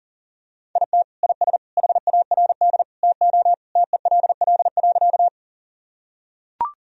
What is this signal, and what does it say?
Morse code, heard as a keyed tone, about 40 wpm, 700 Hz. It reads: IT IS HARD TO TELL.
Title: Sentences from Top 1000 Words - Review Part 1 - 40wpm